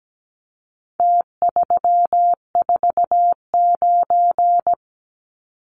Morse code, heard as a keyed tone, about 17 wpm, 700 Hz. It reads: T349
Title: Numbers - Contest Style - 4-Digit 17wpm